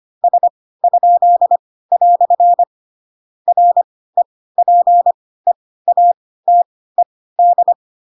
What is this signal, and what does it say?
Morse code, heard as a keyed tone, about 25 wpm, spaced as for 16 wpm, 700 Hz. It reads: S?" REPEATED
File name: Pinocchio / Pinocchio ch7 10 25 ik1hge